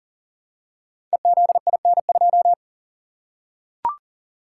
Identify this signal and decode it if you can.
Morse code, heard as a keyed tone, about 38 wpm, 700 Hz. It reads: EZIN2
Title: Code Groups - 5 Letter:Numbers - 40wpm